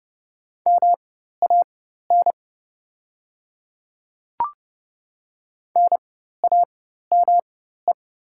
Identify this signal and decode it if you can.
Morse code, heard as a keyed tone, about 30 wpm, spaced as for 14 wpm, 700 Hz. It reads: MAN NAME
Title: 4x Character Spacing - Top 300 Words - 30wpm